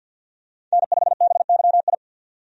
Morse code, heard as a keyed tone, about 51 wpm, 700 Hz. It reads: N5BYI
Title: USA Call Signs - 5 Characters - 50wpm